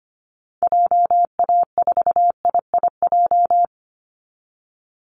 Morse code, heard as a keyed tone, about 25 wpm, 700 Hz. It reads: JA4IIJ
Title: Call Signs Repeated 3x - 25wpm